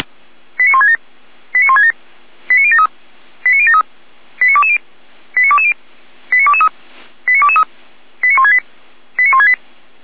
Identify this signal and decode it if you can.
Five-tone selective call (ZVEI), heard as quick runs of five tones, tones of about 70 ms. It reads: 89167 89167 89073 89073 89220 89220 89223 89223 89168 89168